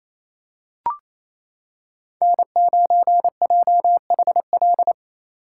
Morse code, heard as a keyed tone, about 27 wpm, 700 Hz. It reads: N9JHL
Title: K1USN SST QSO Practice - 28wpm